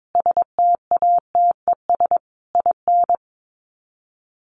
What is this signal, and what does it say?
Morse code, heard as a keyed tone, about 22 wpm, 700 Hz. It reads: STATES IN
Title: Sets of 5 Words 22wpm